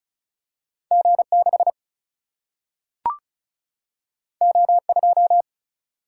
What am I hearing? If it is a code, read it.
Morse code, heard as a keyed tone, about 36 wpm, 700 Hz. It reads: G 6 O 2
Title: Single-Letter Single-Number 35wpm